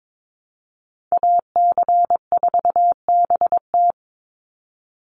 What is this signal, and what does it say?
Morse code, heard as a keyed tone, about 22 wpm, 700 Hz. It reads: AC4BT
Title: Call Signs - 22wpm